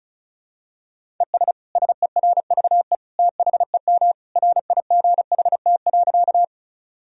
Morse code, heard as a keyed tone, about 36 wpm, 700 Hz. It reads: ES SERVE THEM RIGHT.
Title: Sentences from Top 400 Words and Abbreviations 35wpm